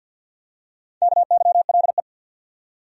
Morse code, heard as a keyed tone, about 48 wpm, 700 Hz. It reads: KYLE
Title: USA Common Names - 50wpm